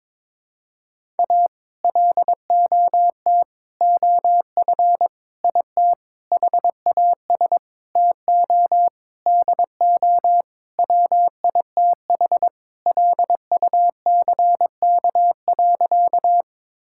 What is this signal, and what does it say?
Morse code, heard as a keyed tone, about 22 wpm, 700 Hz. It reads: A LOT OF IT HAS TO DO WITH LUCK.
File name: Sentences with 7-Word Phrases - 22wpm